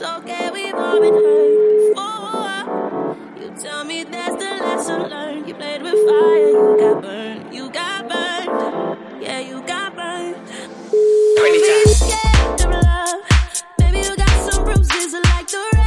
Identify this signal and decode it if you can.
Ringback tone (Europe, 425 Hz), with noise.